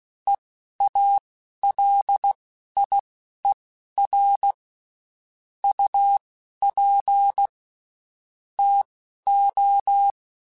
Morse code, heard as a keyed tone, about 16 wpm, spaced as for 11 wpm, 800 Hz. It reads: EALIER UP TO
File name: Spaced Characters / Spaced Characters-mp3s / Weather 5 5 sp @15